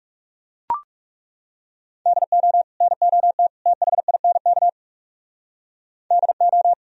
Morse code, heard as a keyed tone, about 37 wpm, 690 Hz. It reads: DO NOT THINK DO
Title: Speed-Racing - Sets of 3 Words 30wpm